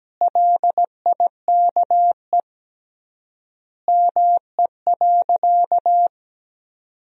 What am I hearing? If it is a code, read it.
Morse code, heard as a keyed tone, about 17 wpm, 700 Hz. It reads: LIKE ME.
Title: 3x Word-Spacing - Sentences from Top 200 Words - 17wpm